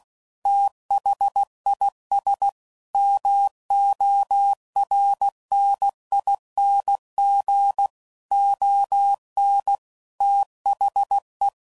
Morse code, heard as a keyed tone, about 16 wpm, 800 Hz. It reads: THIS MORNING ON THE
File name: CW Code Groups @15WPM / Weather Sets / Weather 4 24